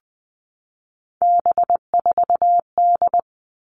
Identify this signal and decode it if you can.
Morse code, heard as a keyed tone, about 20 wpm, 700 Hz. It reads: B4D
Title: All Cut Numbers - 3-Digit 20wpm